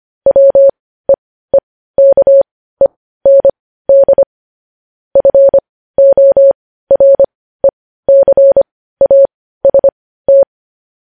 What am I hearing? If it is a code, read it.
Morse code, heard as a keyed tone, about 25 wpm, spaced as for 15 wpm, 550 Hz. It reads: WEEKEND FORECAST
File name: podcast / News headlines-1x-25x15